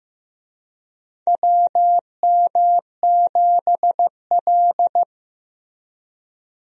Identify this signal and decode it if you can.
Morse code, heard as a keyed tone, about 15 wpm, 700 Hz. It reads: WM7L
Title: USA Call Signs - 4 Characters - Repeated 3x - 15wpm